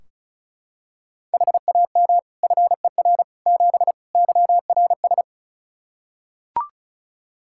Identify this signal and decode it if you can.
Morse code, heard as a keyed tone, about 36 wpm, 700 Hz. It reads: HAM FER 7 YRS
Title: Practice QSO - Exchange 3 - More Info - 35wpm